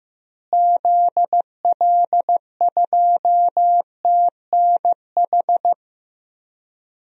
Morse code, heard as a keyed tone, about 15 wpm, 700 Hz. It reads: ZL2TNH
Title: DXCC-100 Call Signs - 6 Characters - Repeated 2x - 15wpm